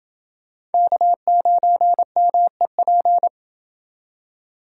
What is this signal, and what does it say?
Morse code, heard as a keyed tone, about 28 wpm, 700 Hz. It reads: K9MEP